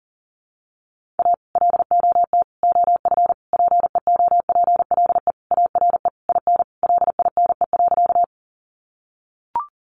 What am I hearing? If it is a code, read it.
Morse code, heard as a keyed tone, about 40 wpm, 700 Hz. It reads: A LOT OF PEOPLE ARE IN LINE.
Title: Additional Easy Phrases 40wpm